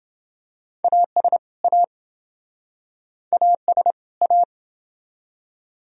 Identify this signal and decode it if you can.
Morse code, heard as a keyed tone, about 28 wpm, 700 Hz. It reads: AS A AS A